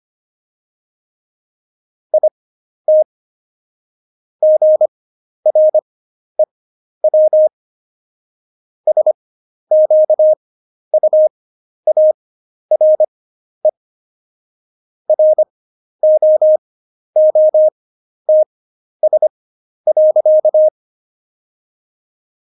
Morse code, heard as a keyed tone, about 25 wpm, spaced as for 11 wpm, 620 Hz. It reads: IT GREW SQUARE ROOTS.
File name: Jokes-25-char-10-eff